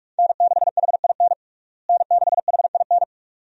Morse code, heard as a keyed tone, about 42 wpm, 700 Hz. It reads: N6HIN N6HIN
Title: USA Call Signs - 5 Characters - Repeated 2x - 45wpm